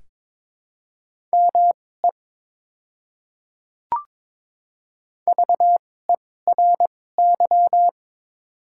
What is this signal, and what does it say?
Morse code, heard as a keyed tone, about 22 wpm, spaced as for 16 wpm, 700 Hz. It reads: ME VERY
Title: 2x Character Spacing - Top 500 Words - 22wpm